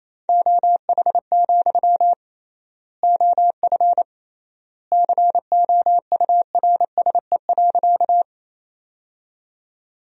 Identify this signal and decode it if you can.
Morse code, heard as a keyed tone, about 28 wpm, 700 Hz. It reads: OH, OF COURSE.
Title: 3x Word-Spacing - Sentences from Top 500 Words - 28wpm